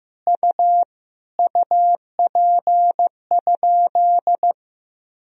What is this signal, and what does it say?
Morse code, heard as a keyed tone, about 15 wpm, 700 Hz. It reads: U UP?
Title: Sentences from Top 500 Words and Abbreviations 15wpm